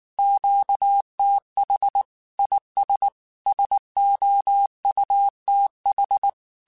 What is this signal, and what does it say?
Morse code, heard as a keyed tone, about 19 wpm, 800 Hz. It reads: QTH IS SOUTH
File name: QSOs / QSOs-mp3s / WA7QOX 2@18